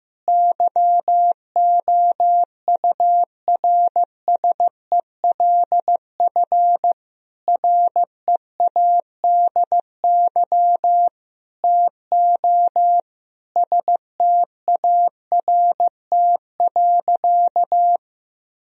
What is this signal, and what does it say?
Morse code, heard as a keyed tone, about 15 wpm, 700 Hz. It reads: YOURSELF READY TO START.
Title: Sentences from Top 1000 Words - Review Part 1 - 15wpm